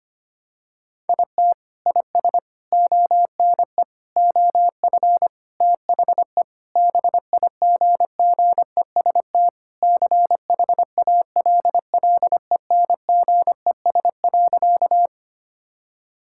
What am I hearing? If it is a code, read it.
Morse code, heard as a keyed tone, about 25 wpm, 700 Hz. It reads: IT IS ONE OF THE BIGGEST CHALLENGES.
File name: practice / 25wpm / Sentences with 4-Word Phrases - 25wpm